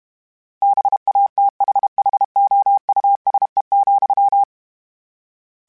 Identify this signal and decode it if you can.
Morse code, heard as a keyed tone, about 32 wpm, 800 Hz. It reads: BATHHOUSE,
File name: General / General-mp3s / Houses @30